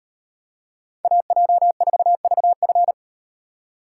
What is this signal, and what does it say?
Morse code, heard as a keyed tone, about 38 wpm, 700 Hz. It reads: AJ4VF